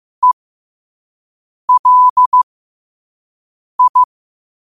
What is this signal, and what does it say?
Morse code, heard as a keyed tone, about 15 wpm, spaced as for 5 wpm, 1000 Hz. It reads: ELI
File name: qotdfiles / 5wpm